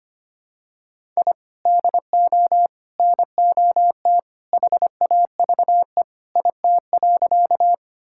Morse code, heard as a keyed tone, about 25 wpm, 700 Hz. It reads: I DO NOT HAVE IT.
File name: practice / 25wpm / Sentences with 2-Word Phrases - 25wpm